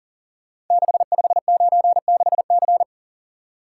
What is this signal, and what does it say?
Morse code, heard as a keyed tone, about 38 wpm, 700 Hz. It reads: 6596C
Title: Code Groups - 5 Letter:Numbers - 40wpm